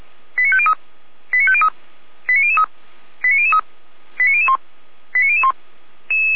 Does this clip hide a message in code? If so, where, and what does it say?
Yes: five-tone selective call (ZVEI) from 0.4 to 5.5 s, heard as quick runs of five tones, tones of about 70 ms. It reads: 89503 89503 89003 89003 89001 89001